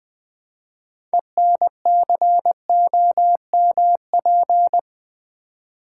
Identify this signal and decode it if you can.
Morse code, heard as a keyed tone, about 21 wpm, 700 Hz. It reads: ENCOMP
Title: Build A Word - 3 Letters At A Time - 9-Letter Words - 20wpm